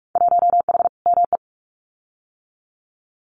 Morse code, heard as a keyed tone, about 46 wpm, 700 Hz. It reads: 1H ME
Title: Winter Field Day Exchanges 45wpm